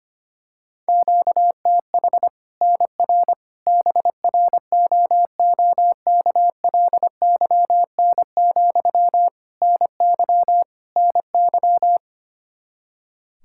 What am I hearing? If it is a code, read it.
Morse code, heard as a keyed tone, about 25 wpm, 700 Hz. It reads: QTH NR BROOKLYN, NY NY
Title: Practice QSO - Exchange 1 - Beginning - 25wpm